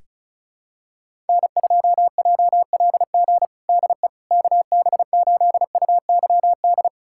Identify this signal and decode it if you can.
Morse code, heard as a keyed tone, about 36 wpm, 700 Hz. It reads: N2JLG DE KB8UYD
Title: USA Callsign DE Callsign 35wpm